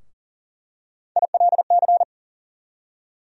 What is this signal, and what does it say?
Morse code, heard as a keyed tone, about 40 wpm, 700 Hz. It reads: ILC
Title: USA Call Sign Postfixes - 3 Letters 40wpm